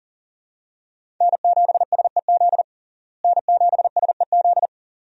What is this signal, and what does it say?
Morse code, heard as a keyed tone, about 40 wpm, 700 Hz. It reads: N7SEZ N7SEZ
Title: USA Call Signs - 5 Characters - Repeated 2x - 40wpm